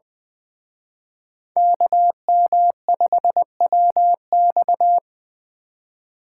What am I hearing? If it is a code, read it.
Morse code, heard as a keyed tone, about 20 wpm, 700 Hz. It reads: KM5WX
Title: Speed Racing - USA 2x2 Call Signs 17wpm